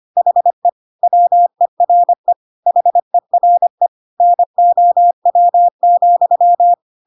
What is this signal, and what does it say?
Morse code, heard as a keyed tone, about 25 wpm, 700 Hz. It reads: HE WERE HERE NOW,
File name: Pinocchio / Pinocchio ch5 25 25 ik1hge